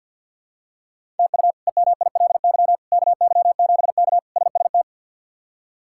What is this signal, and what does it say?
Morse code, heard as a keyed tone, about 51 wpm, 700 Hz. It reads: TU EMILY KY7K SST
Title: K1USN SST QSO Practice - 50wpm